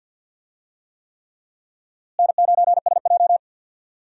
Morse code, heard as a keyed tone, about 51 wpm, 700 Hz. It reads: N9SJ
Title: USA Call Signs - 4 Characters - Repeated 3x - 50wpm